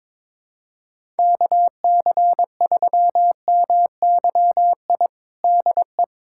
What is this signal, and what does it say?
Morse code, heard as a keyed tone, about 22 wpm, 700 Hz. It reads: KC3MYI DE
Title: USA Callsign DE Callsign - 6 Characters - 22wpm